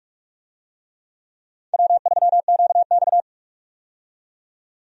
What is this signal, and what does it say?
Morse code, heard as a keyed tone, about 45 wpm, 700 Hz. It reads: W3QX
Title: USA Call Signs - 4 Characters - 45wpm